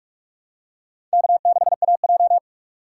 Morse code, heard as a keyed tone, about 44 wpm, 700 Hz. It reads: K6AJ
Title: USA Call Signs - 2x Character Spacing - 45wpm